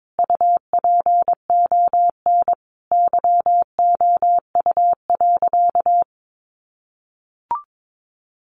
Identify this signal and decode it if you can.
Morse code, heard as a keyed tone, about 22 wpm, 700 Hz. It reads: UPON YOU.
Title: Sentences from Top 1500 Words 22wpm